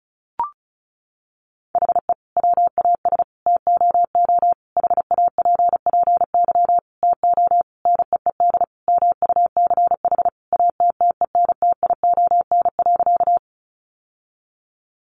Morse code, heard as a keyed tone, about 34 wpm, 700 Hz. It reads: HE WAS TOO HAPPY TO NEED MUCH ATTENTION.
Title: Sentences from Top 1000 Words 35wpm